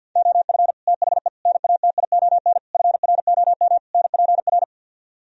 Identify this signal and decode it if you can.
Morse code, heard as a keyed tone, about 61 wpm, 700 Hz. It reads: OF THE NATION FROM NPR
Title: Speed-Racing - Sets of 5 Words 50wpm